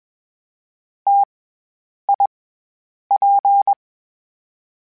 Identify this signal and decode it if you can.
Morse code, heard as a keyed tone, about 20 wpm, 800 Hz. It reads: T I P
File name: Variable Speed / Variable Speed-mp3s / T AOEI's sp @20